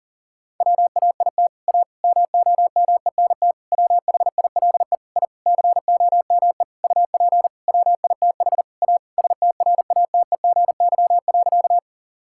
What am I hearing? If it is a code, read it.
Morse code, heard as a keyed tone, about 40 wpm, 700 Hz. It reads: WAIT A MOMENT WHILE I COME UP WITH A STRATEGY.
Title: Sentences from Top 1000 Words - Review Part 5 - 40wpm